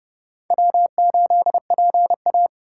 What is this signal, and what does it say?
Morse code, heard as a keyed tone, about 28 wpm, 700 Hz. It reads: W8PA